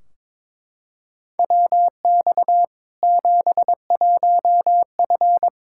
Morse code, heard as a keyed tone, about 22 wpm, 700 Hz. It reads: WX 71F
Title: K1USN SST QSO Practice 22wpm